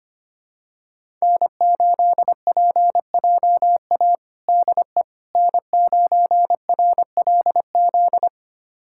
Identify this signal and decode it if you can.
Morse code, heard as a keyed tone, about 25 wpm, 700 Hz. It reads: N8PJA DE N9RLZ
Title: USA Callsign DE Callsign - 5 Characters - 25wpm